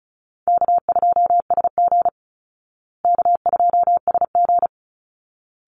Morse code, heard as a keyed tone, about 34 wpm, 700 Hz. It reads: K2SG K2SG
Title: Easy Call Signs Repeated 2x - 35wpm